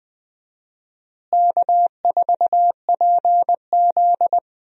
Morse code, heard as a keyed tone, about 20 wpm, 700 Hz. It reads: K4PZ
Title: USA Call Signs - 4 Characters - Repeated 3x - 20wpm